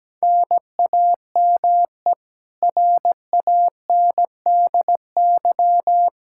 Morse code, heard as a keyed tone, about 17 wpm, 700 Hz. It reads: NAME RANDY
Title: Practice QSO - Exchange 1 - Beginning - 17wpm